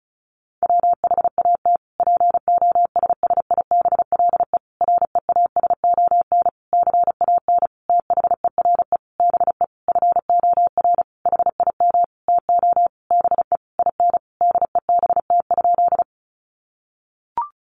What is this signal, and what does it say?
Morse code, heard as a keyed tone, about 34 wpm, 700 Hz. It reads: WHAT POSSIBLE REASON CAN THERE BE FOR HIM TO BE IN DEBT?
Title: Sentences from Top 1300 Words 35wpm